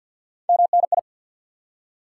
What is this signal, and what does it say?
Morse code, heard as a keyed tone, about 47 wpm, 700 Hz. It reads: MNI
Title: Top 90 QSO Elements 50wpm